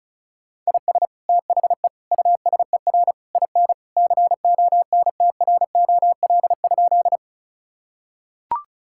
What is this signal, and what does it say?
Morse code, heard as a keyed tone, about 36 wpm, 700 Hz. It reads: IS THE USER IN CONTROL?
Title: Sentences from Top 1000 Words - Review Part 5 - 35wpm